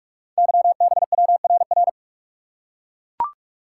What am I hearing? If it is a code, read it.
Morse code, heard as a keyed tone, about 44 wpm, 700 Hz. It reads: YBWRR
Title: Code Groups - 5 Letters - 45wpm